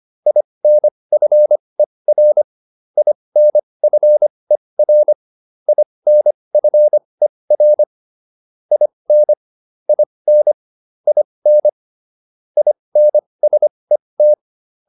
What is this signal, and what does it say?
Morse code, heard as a keyed tone, about 25 wpm, spaced as for 20 wpm, 600 Hz. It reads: INFER INFER INFER IN IN IN INSET